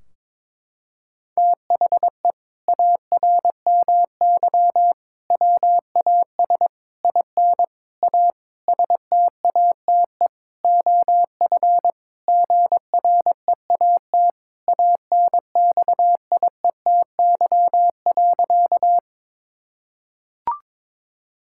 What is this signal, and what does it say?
Morse code, heard as a keyed tone, about 22 wpm, 700 Hz. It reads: THE ARMY WAS IN A STATE OF GREAT ANXIETY.
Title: Sentences from Top 1600 Words 22wpm